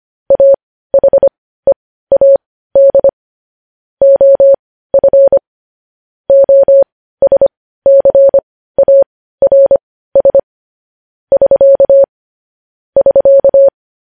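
Morse code, heard as a keyed tone, about 25 wpm, spaced as for 15 wpm, 550 Hz. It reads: AHEAD OF OSCARS <SK> <SK>
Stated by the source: Morse practice of news headlines at 15 wpm.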